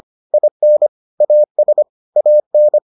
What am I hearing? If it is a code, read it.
Morse code, heard as a keyed tone, about 25 wpm, 600 Hz. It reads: IN AS AN